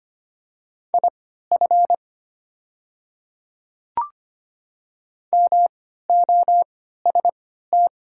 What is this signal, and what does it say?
Morse code, heard as a keyed tone, about 25 wpm, spaced as for 14 wpm, 700 Hz. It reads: IF MOST
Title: 3x Character Spacing - Top 100 Words - 25wpm